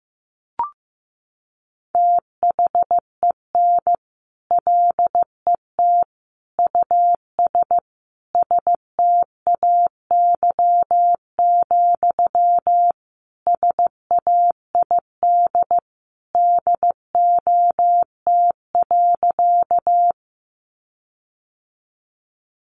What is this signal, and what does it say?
Morse code, heard as a keyed tone, about 15 wpm, 700 Hz. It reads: THEN LET US STAY, SAID DOT.
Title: Sentences from Top 1500 Words 15wpm